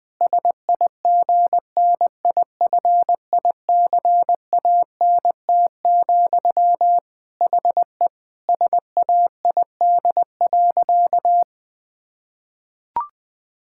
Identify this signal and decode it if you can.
Morse code, heard as a keyed tone, about 20 wpm, 700 Hz. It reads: SIGNIFICANT, HE SAID.
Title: Sentences from Top 1000 Words - Review Part 6 - 20wpm